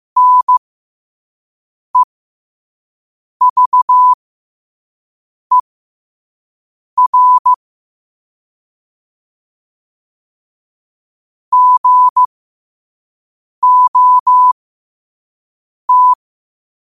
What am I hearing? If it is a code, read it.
Morse code, heard as a keyed tone, about 15 wpm, spaced as for 5 wpm, 1000 Hz. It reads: NEVER GOT
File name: 5wpm